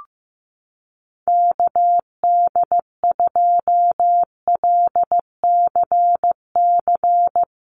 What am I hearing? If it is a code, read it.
Morse code, heard as a keyed tone, about 15 wpm, 700 Hz. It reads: KD2LCC
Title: USA Call Signs Repeated 2x 15wpm